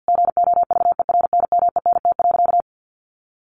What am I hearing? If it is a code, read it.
Morse code, heard as a keyed tone, about 50 wpm, 700 Hz. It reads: GOVERNMENT.
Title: Sentences from Top 1400 Words 50wpm